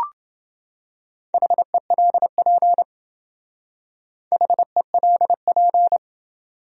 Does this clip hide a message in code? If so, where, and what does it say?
Yes: Morse code from 1.3 to 6.0 s, heard as a keyed tone, about 28 wpm, 700 Hz. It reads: HELP HELP